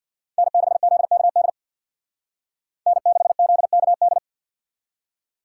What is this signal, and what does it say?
Morse code, heard as a keyed tone, about 64 wpm, 700 Hz. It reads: N6ZKD N6ZKD